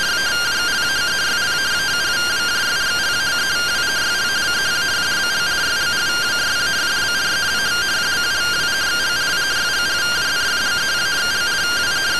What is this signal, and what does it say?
Two-tone teleprinter signal, about 170 Hz shift.
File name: findings / 81 axial iodine 262.gen-17 rtty rx